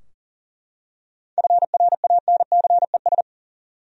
Morse code, heard as a keyed tone, about 40 wpm, 700 Hz. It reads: FRANCES
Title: USA Common Names - 40wpm